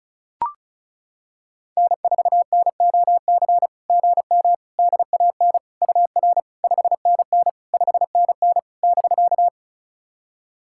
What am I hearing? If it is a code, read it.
Morse code, heard as a keyed tone, about 34 wpm, 690 Hz. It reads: N4NOC GM DAN UR 5NN 5NN <BK>
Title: SOTA QSO Practice 35wpm